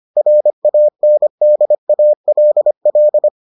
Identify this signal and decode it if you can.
Morse code, heard as a keyed tone, about 25 wpm, 600 Hz. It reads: RANDALL